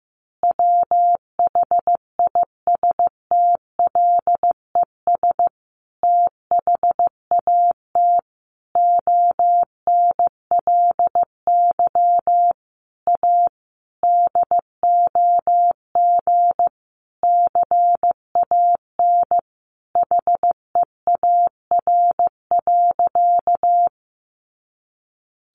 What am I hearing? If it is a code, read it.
Morse code, heard as a keyed tone, about 15 wpm, 700 Hz. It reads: WHISTLES THAT ONLY A DOG CAN HEAR.
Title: A Sound Of Thunder 15wpm